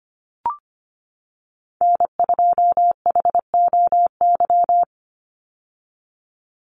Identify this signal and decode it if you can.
Morse code, heard as a keyed tone, about 25 wpm, 700 Hz. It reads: N2HOY
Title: USA Call Signs 25wpm